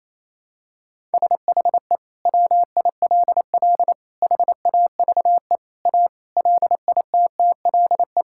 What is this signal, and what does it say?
Morse code, heard as a keyed tone, about 27 wpm, 700 Hz. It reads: SHE WILL HAVE A LITTLE
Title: Sentences from Top 1000 Words - Review Part 5 - 28wpm